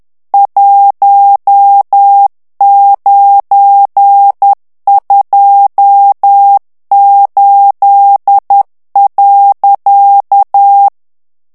Morse code, heard as a keyed tone, about 10 wpm, 800 Hz. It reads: 1928.
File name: ITALIEN texte clair 1 vitesse 10 mots minute